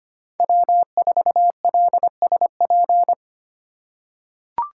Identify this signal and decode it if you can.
Morse code, heard as a keyed tone, about 25 wpm, 700 Hz. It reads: W4LSP
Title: USA Call Signs - 6x Character Spacing - 25wpm